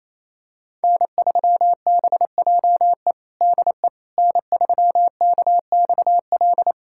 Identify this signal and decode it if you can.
Morse code, heard as a keyed tone, about 27 wpm, 700 Hz. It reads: N3BJE DE N3KXL
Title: USA Callsign DE Callsign - 5 Characters - 28wpm